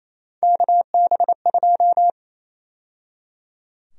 Morse code, heard as a keyed tone, about 28 wpm, 700 Hz. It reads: KB2
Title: USA Call Sign Prefixes - 28wpm